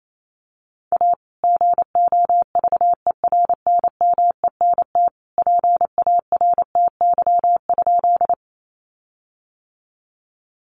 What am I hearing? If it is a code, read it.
Morse code, heard as a keyed tone, about 28 wpm, 700 Hz. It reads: A GOVERNMENT PARTY?
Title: Sentences from Top 1400 Words 28wpm